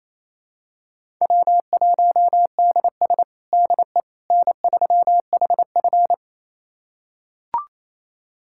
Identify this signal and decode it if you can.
Morse code, heard as a keyed tone, about 28 wpm, 700 Hz. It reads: W1DS DE N3HF
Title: USA Callsign DE Callsign - 4 Characters - 28wpm